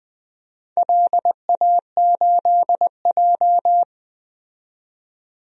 Morse code, heard as a keyed tone, about 20 wpm, 700 Hz. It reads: LA8J
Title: DXCC-100 Call Signs - 4 Characters - 20wpm